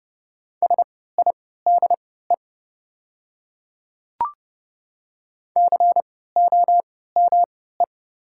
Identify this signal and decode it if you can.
Morse code, heard as a keyed tone, about 30 wpm, spaced as for 17 wpm, 700 Hz. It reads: SIDE COME
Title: 3x Character Spacing - Top 100 Words - 30wpm